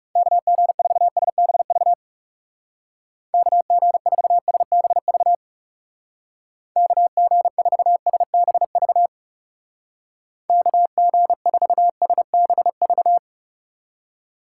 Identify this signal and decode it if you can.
Morse code, heard as a keyed tone, about 36 wpm, 700 Hz. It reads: KG4SBV KG4SBV KG4SBV KG4SBV
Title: Speed Racing - USA 2x3 Call Signs 30wpm